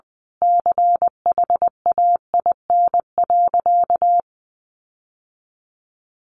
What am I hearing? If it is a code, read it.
Morse code, heard as a keyed tone, about 20 wpm, 700 Hz. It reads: CHAIN.
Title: Sentences from Top 1500 Words 20wpm